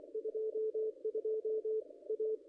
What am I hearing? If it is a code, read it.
Morse code, heard as a keyed tone, about 24 wpm, 440 Hz, with radio noise. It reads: 22 A
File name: contests / ss-cw-k5tr-op-n5rz -0200